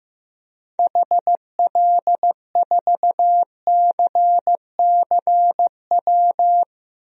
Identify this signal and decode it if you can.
Morse code, heard as a keyed tone, about 15 wpm, 700 Hz. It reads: HL4CCW
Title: DXCC-100 Call Signs - 6 Characters - Repeated 3x - 15wpm